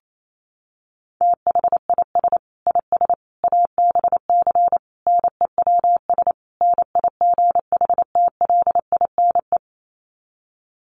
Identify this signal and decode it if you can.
Morse code, heard as a keyed tone, about 28 wpm, 700 Hz. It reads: THIS IS ABC NEWS NIGHTLINE
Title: Sets of 5 Words 28wpm